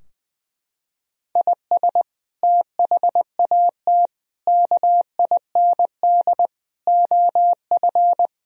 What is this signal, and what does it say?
Morse code, heard as a keyed tone, about 20 wpm, 700 Hz. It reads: IS THAT KIND OF
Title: Sentences from Top 1000 Words - Review Part 3 - 20wpm